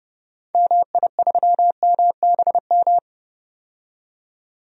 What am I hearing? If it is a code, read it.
Morse code, heard as a keyed tone, about 30 wpm, 700 Hz. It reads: MI3MBM
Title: DXCC-100 Call Signs - 6 Characters - 30wpm